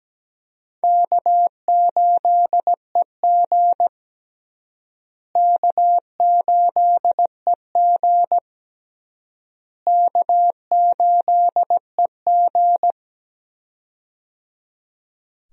Morse code, heard as a keyed tone, about 17 wpm, 700 Hz. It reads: K8EG K8EG K8EG
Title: USA Call Signs - 4 Characters - Repeated 3x - 17wpm